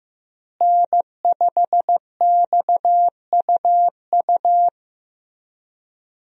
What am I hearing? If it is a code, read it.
Morse code, heard as a keyed tone, about 15 wpm, 700 Hz. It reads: N5XUU
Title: USA Call Signs - 5 Characters - Repeated 2x - 15wpm